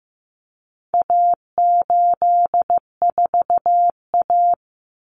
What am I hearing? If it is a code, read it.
Morse code, heard as a keyed tone, about 15 wpm, 700 Hz. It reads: A84A
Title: Common Cut Numbers - 4-Digit 15wpm